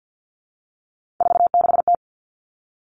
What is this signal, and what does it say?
Morse code, heard as a keyed tone, about 50 wpm, 700 Hz. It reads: 46T
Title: All Cut Numbers - 3-Digit 50wpm